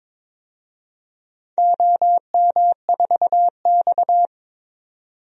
Morse code, heard as a keyed tone, about 23 wpm, 700 Hz. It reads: OM4X
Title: DXCC-100 Call Signs - 4 Characters - 22wpm